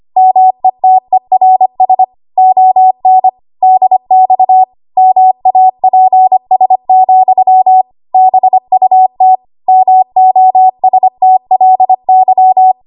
Morse code, heard as a keyed tone, about 25 wpm, 750 Hz. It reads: METERS ON DX MAPS, BUT MOSTLY
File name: Morse / 25WPM